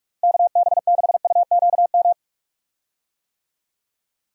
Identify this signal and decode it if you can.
Morse code, heard as a keyed tone, about 44 wpm, 700 Hz. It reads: KB6UQM
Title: USA Call Signs - 6 Characters - Repeated 2x - 45wpm